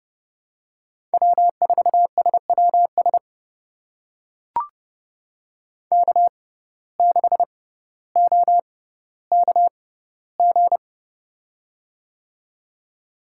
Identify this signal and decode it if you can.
Morse code, heard as a keyed tone, about 30 wpm, spaced as for 15 wpm, 700 Hz. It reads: W4SWS K6OKG
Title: USA Call Signs - 5 Characters - 6x Character Spacing - 30wpm